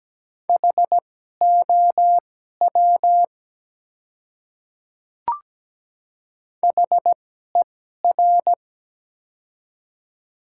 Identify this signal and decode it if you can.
Morse code, heard as a keyed tone, about 17 wpm, spaced as for 12 wpm, 700 Hz. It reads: HOW HER
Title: 2x Character Spacing - Top 100 Words - 17wpm